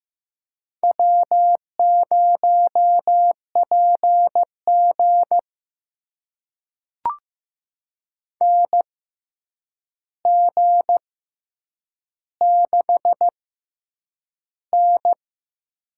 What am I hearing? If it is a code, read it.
Morse code, heard as a keyed tone, about 15 wpm, spaced as for 8 wpm, 700 Hz. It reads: W0PG NG6N
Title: USA Call Signs - 4 Characters - 6x Character Spacing - 15wpm